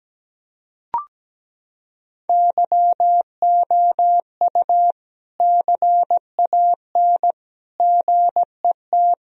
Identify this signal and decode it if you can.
Morse code, heard as a keyed tone, about 17 wpm, 700 Hz. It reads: YOU CAN GET
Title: Sentences from Top 1000 Words - Review Part 2 - 17wpm